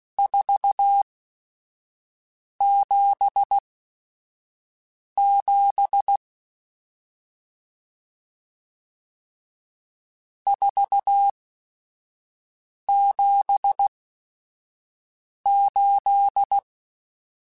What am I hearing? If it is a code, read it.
Morse code, heard as a keyed tone, about 16 wpm, spaced as for 5 wpm, 800 Hz. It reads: 477 478
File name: RSTs / RST-mp3s / Various RST @5